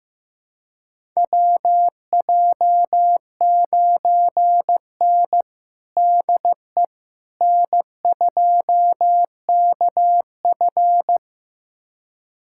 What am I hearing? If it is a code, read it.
Morse code, heard as a keyed tone, about 16 wpm, 700 Hz. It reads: WJ9N DE N2KF
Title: USA Callsign DE Callsign - 4 Characters - 15wpm